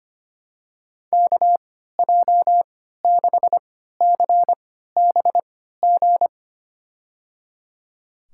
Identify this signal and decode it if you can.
Morse code, heard as a keyed tone, about 25 wpm, spaced as for 14 wpm, 700 Hz. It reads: KJ6CBG
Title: USA Call Signs - 6 Characters - 3x Character Spacing - 25wpm